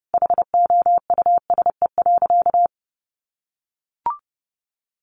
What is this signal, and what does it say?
Morse code, heard as a keyed tone, about 30 wpm, 700 Hz. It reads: HOUSE.
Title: Sentences from Top 1100 Words 30wpm